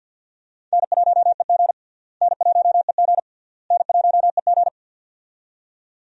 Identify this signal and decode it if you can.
Morse code, heard as a keyed tone, about 52 wpm, 700 Hz. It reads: N1EG N1EG N1EG
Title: USA Call Signs - 4 Characters - Repeated 3x - 50wpm